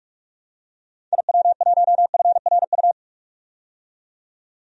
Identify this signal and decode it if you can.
Morse code, heard as a keyed tone, about 45 wpm, 700 Hz. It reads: IW1FRU
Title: DXCC-100 Call Signs - 6 Characters - 45wpm